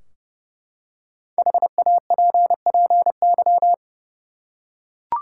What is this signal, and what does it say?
Morse code, heard as a keyed tone, about 30 wpm, 700 Hz. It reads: HAPPY
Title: Five Letter Words - 30wpm